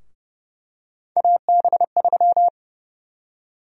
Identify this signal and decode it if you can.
Morse code, heard as a keyed tone, about 31 wpm, 700 Hz. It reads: AB3
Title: USA Call Sign Prefixes - 30wpm